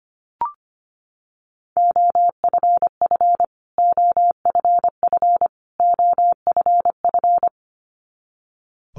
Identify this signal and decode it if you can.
Morse code, heard as a keyed tone, about 25 wpm, 700 Hz. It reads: OFF OFF OFF
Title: Top 300 Words Repeated 25wpm